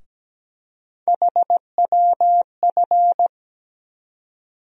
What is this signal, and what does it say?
Morse code, heard as a keyed tone, about 17 wpm, 700 Hz. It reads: HWF
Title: USA Call Sign Postfixes - 3 Letters 17wpm